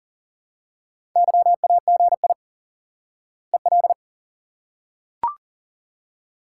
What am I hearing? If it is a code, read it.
Morse code, heard as a keyed tone, about 38 wpm, 700 Hz. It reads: YAGI EL
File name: IQE Course 3 - Lesson 89 - YAGI - Two Elements - 40wpm